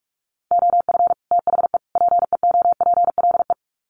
Morse code, heard as a keyed tone, about 44 wpm, 700 Hz. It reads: OF THE PEOPLE
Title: Sets of 3 Words 45wpm